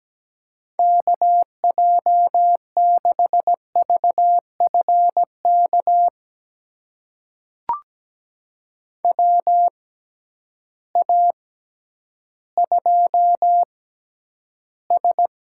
Morse code, heard as a keyed tone, about 17 wpm, spaced as for 10 wpm, 700 Hz. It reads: KJ6VFK WA2S
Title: USA Call Signs - 6x Character Spacing - 17wpm